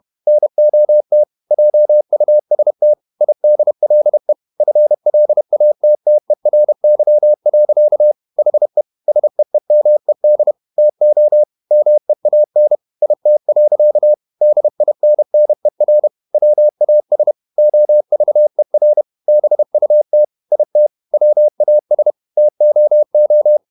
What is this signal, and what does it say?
Morse code, heard as a keyed tone, about 31 wpm, 600 Hz. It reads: NOT JUST IDLE FLATTERY. HE SEEMED TO MEAN IT. DINNER WAS OVER BUT IT WAS TOO
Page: Never 31wpm | CW med Gnister